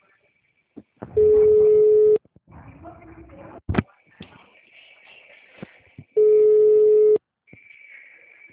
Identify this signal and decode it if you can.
Ringback tone (Europe, 425 Hz).
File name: out- -2011- - - .22881